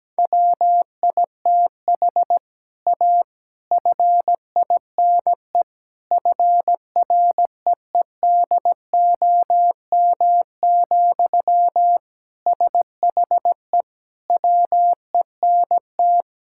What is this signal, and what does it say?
Morse code, heard as a keyed tone, about 17 wpm, 700 Hz. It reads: WITH A FINE FREEDOM, SHE WENT
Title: Sentences from Top 1000 Words - Review Part 5 - 17wpm